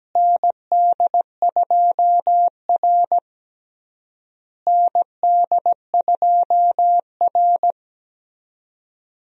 Morse code, heard as a keyed tone, about 17 wpm, 700 Hz. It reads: ND2R ND2R
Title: USA Call Signs - 4 Characters - Repeated 3x - 17wpm